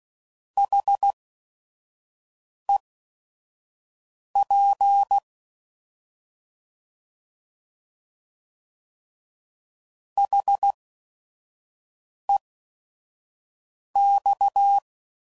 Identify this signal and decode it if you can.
Morse code, heard as a keyed tone, about 16 wpm, spaced as for 5 wpm, 800 Hz. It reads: HEP HEX